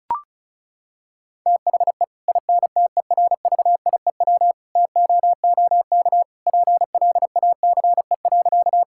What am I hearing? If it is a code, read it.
Morse code, heard as a keyed tone, about 34 wpm, 700 Hz. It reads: THE INTERVIEW TOOK PLACE.
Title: Sentences from Top 1600 Words 35wpm